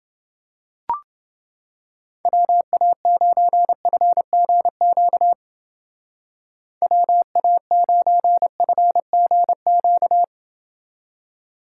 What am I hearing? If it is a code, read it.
Morse code, heard as a keyed tone, about 29 wpm, 700 Hz. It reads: WA9FGQ WA9FGQ